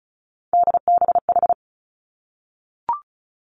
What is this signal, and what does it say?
Morse code, heard as a keyed tone, about 36 wpm, 700 Hz. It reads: DBH
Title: Three Challenging Letter-Number Combinations 35wpm